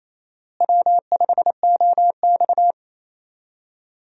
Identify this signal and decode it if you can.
Morse code, heard as a keyed tone, about 28 wpm, 700 Hz. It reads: W5OX
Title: USA Call Signs - 4 Characters - Repeated 2x - 28wpm